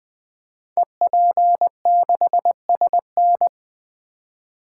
Morse code, heard as a keyed tone, about 20 wpm, 700 Hz. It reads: EP6SN
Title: Code Groups - 5 Letter:Numbers - 20wpm